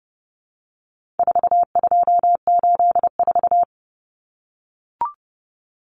Morse code, heard as a keyed tone, about 30 wpm, 700 Hz. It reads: 4284
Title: Common Cut Numbers - 4-Digit 30wpm